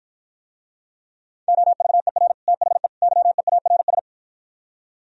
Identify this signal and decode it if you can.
Morse code, heard as a keyed tone, about 52 wpm, 700 Hz. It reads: OVER THE YEARS